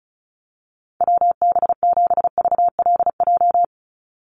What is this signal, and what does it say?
Morse code, heard as a keyed tone, about 34 wpm, 700 Hz. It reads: WB7VLJ
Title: USA Call Signs 35wpm